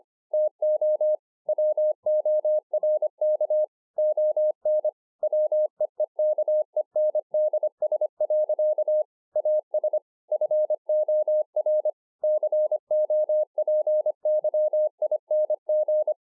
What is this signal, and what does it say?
Morse code, heard as a keyed tone, about 25 wpm, 610 Hz. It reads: TO WORK ON WEEKENDS. AS FOR COPYING
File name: 2011-04-mp3 / THIS IS NOT MY SHACK 2 25WPM